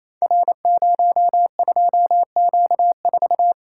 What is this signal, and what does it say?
Morse code, heard as a keyed tone, about 29 wpm, 700 Hz. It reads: R02Q4
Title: Code Groups - 5 Letter:Numbers - 28wpm